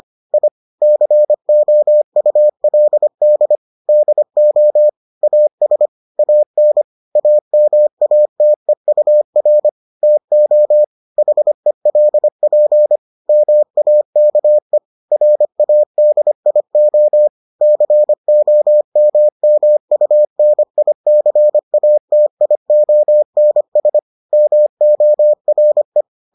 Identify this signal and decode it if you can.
Morse code, heard as a keyed tone, about 25 wpm, 600 Hz. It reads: I COULD DO AS AN AMATEUR TO HELP MAKE RADIO COMMUNICATIONS MORE